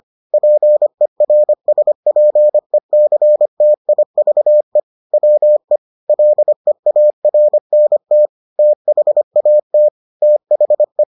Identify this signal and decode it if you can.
Morse code, heard as a keyed tone, about 25 wpm, 600 Hz. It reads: PERSPECTIVE WE LEARNT THAT THE